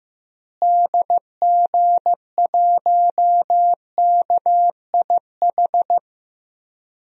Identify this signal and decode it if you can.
Morse code, heard as a keyed tone, about 15 wpm, 700 Hz. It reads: DG1KIH
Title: DXCC-100 Call Signs - 6 Characters - Repeated 3x - 15wpm